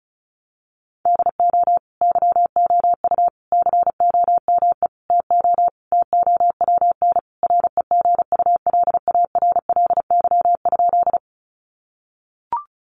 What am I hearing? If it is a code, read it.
Morse code, heard as a keyed tone, about 34 wpm, 700 Hz. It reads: DO YOU COME TO TOWN REGULARLY?
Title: Sentences from Top 1400 Words 35wpm